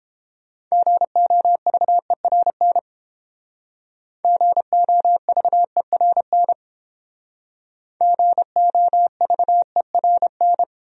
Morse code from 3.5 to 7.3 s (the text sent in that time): GOVERN